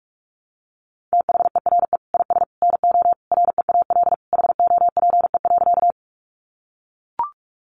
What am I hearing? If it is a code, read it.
Morse code, heard as a keyed tone, about 44 wpm, 700 Hz. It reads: THERE IS NO REAL HOPE.
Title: Sentences from Top 1000 Words 45wpm